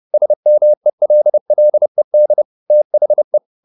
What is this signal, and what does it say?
Morse code, heard as a keyed tone, about 30 wpm, 600 Hz. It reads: SMELLED THE